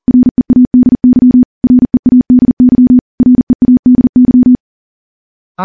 Morse code, heard as a keyed tone, about 40 wpm, 250 Hz. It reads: READY READY READY
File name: morse-training-40WPM